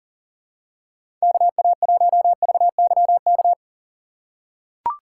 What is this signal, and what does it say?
Morse code, heard as a keyed tone, about 40 wpm, 700 Hz. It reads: KA1VYK
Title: USA Call Signs - 6 Characters - 3x Character Spacing - 40wpm